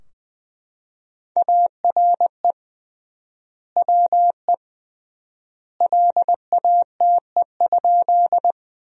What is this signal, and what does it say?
Morse code, heard as a keyed tone, about 21 wpm, 700 Hz. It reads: ARE WE LATE?
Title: 3x Word-Spacing - Sentences from Top 300 Words - 20wpm